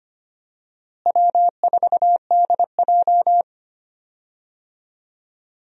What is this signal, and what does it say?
Morse code, heard as a keyed tone, about 25 wpm, 700 Hz. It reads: W4DJ
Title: USA Call Signs - 4 Characters - Repeated 3x - 25wpm